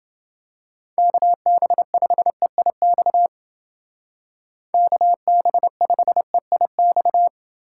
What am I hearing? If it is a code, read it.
Morse code, heard as a keyed tone, about 29 wpm, 700 Hz. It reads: KB5EIX KB5EIX